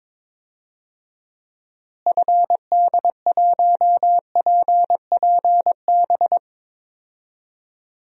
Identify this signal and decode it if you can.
Morse code, heard as a keyed tone, about 22 wpm, 700 Hz. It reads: FD1PPB
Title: DXCC-100 Call Signs - 6 Characters - Repeated 3x - 22wpm